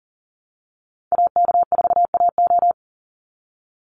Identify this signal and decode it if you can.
Morse code, heard as a keyed tone, about 40 wpm, 700 Hz. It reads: AK4AO
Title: Call Signs Repeated 3x - 40wpm